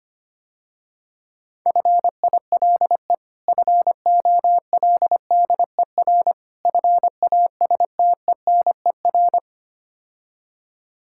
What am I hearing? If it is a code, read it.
Morse code, heard as a keyed tone, about 26 wpm, 700 Hz. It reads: FILE FOLDER FASTENER
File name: practice / 25wpm / Difficult Words - Speed-Racing with Hints - 25wpm